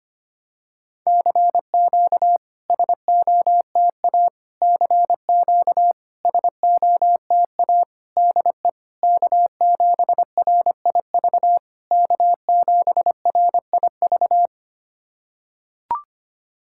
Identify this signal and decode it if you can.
Morse code, heard as a keyed tone, about 25 wpm, 700 Hz. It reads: CQ SOTA CQ SOTA DE K7RIV K7RIV
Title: SOTA QSO Practice 25wpm